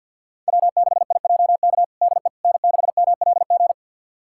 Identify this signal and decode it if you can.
Morse code, heard as a keyed tone, about 51 wpm, 700 Hz. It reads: W6IJK DE N6MLG
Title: USA Callsign DE Callsign - 5 Characters - 50wpm